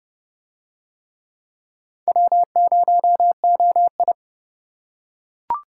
Morse code, heard as a keyed tone, about 28 wpm, 700 Hz. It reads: W0OI